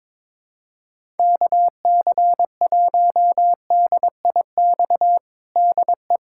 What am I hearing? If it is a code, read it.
Morse code, heard as a keyed tone, about 21 wpm, 700 Hz. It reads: KC1DIX DE
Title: USA Callsign DE Callsign 22wpm